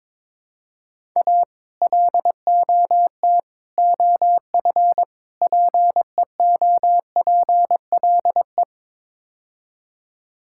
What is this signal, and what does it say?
Morse code, heard as a keyed tone, about 21 wpm, 700 Hz. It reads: A LOT OF PEOPLE
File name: practice / 22wpm / Build A Phrase - 5-Word Phrases - 22wpm